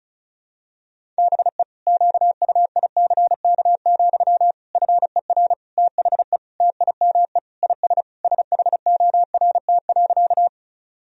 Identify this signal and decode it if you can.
Morse code, heard as a keyed tone, about 36 wpm, 700 Hz. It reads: BE QUICK, FER THE TIME IS SHORT.
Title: Sentences from Top 500 Words and Abbreviations 35wpm